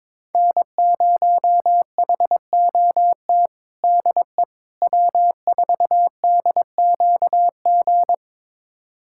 Morse code, heard as a keyed tone, about 22 wpm, 700 Hz. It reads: N0HOT DE W4DQG
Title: USA Callsign DE Callsign - 5 Characters - 22wpm